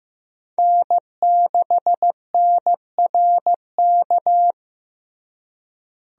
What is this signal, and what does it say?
Morse code, heard as a keyed tone, about 15 wpm, 700 Hz. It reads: N6NRK
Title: USA Call Signs - 5 Characters - Repeated 2x - 15wpm